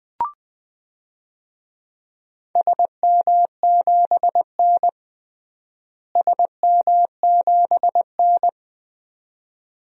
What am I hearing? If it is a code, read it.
Morse code, heard as a keyed tone, about 20 wpm, 700 Hz. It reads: SM7N SM7N
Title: DXCC-100 Call Signs - 4 Characters - Repeated 2x - 20wpm